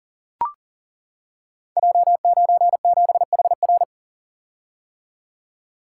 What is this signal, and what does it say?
Morse code, heard as a keyed tone, about 39 wpm, 700 Hz. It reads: J97HR
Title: Code Groups - 5 Letter:Numbers - 40wpm